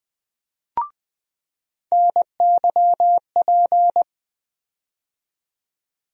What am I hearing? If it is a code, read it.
Morse code, heard as a keyed tone, about 20 wpm, 700 Hz. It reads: NYP
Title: USA Call Sign Postfixes - 3 Letters 20wpm